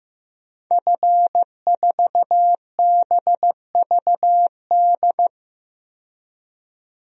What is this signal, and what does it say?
Morse code, heard as a keyed tone, about 15 wpm, 700 Hz. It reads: F4BVD
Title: DXCC-100 Call Signs - 5 Characters - 15wpm